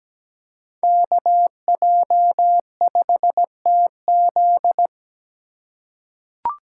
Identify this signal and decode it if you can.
Morse code, heard as a keyed tone, about 17 wpm, 700 Hz. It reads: KJ5TZ
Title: USA Call Signs - 5 Characters - 2x Character Spacing - 17wpm